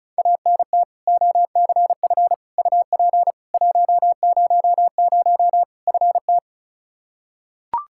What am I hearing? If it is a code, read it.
Morse code, heard as a keyed tone, about 36 wpm, 700 Hz. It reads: ANT OCF UP 100 FT
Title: Practice QSO - Exchange 2 - Info - 35wpm